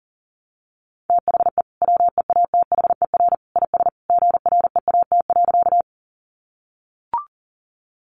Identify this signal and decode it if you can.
Morse code, heard as a keyed tone, about 37 wpm, 700 Hz. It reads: THE WEATHER IS GREAT.
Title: Sentences from Top 800 Words 40wpm